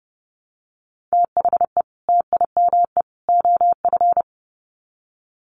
Morse code, heard as a keyed tone, about 30 wpm, 700 Hz. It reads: THE TIME OF
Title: Sets of 3 Words 30wpm